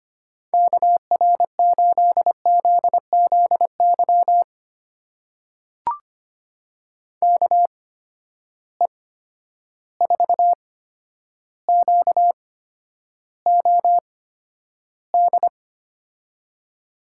Morse code, heard as a keyed tone, about 25 wpm, spaced as for 11 wpm, 700 Hz. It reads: KR8ZZY KE4QOD
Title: USA Call Signs - 6 Characters - 8x Character Spacing - 25wpm